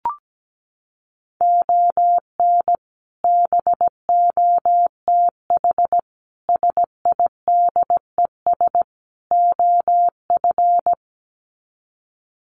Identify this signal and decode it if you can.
Morse code, heard as a keyed tone, about 17 wpm, 700 Hz. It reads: ON BOTH SIDES OF
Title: Sets of 4 Words 17wpm